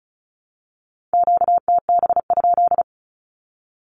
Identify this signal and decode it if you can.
Morse code, heard as a keyed tone, about 34 wpm, 700 Hz. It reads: QTB?
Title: Q-Codes 35wpm